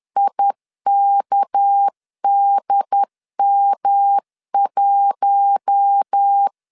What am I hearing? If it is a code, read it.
Morse code, heard as a keyed tone, about 10 wpm, 800 Hz. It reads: IKDM1
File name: LEZIONI / 6 PKX0 / 6 PK 10wpm 2436